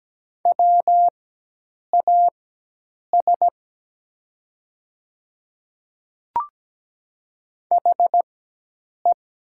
Morse code, heard as a keyed tone, about 17 wpm, spaced as for 8 wpm, 700 Hz. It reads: WAS HE
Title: 4x Character Spacing - Top 100 Words - 17wpm